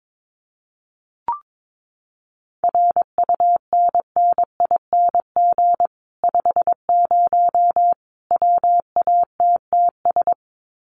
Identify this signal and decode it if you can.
Morse code, heard as a keyed tone, about 22 wpm, 700 Hz. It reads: RUNNING 50 WATTS
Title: Common QSO Phrases 22wpm